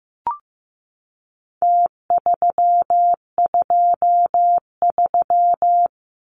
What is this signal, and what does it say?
Morse code, heard as a keyed tone, about 15 wpm, 700 Hz. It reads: T323
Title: Numbers - Contest Style - 4-Digit 15wpm